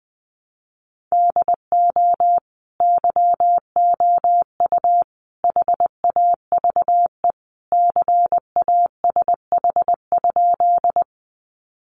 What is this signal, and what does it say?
Morse code, heard as a keyed tone, about 20 wpm, 700 Hz. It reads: DO YOU HAVE CASH?
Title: Sentences from Top 700 Words 20wpm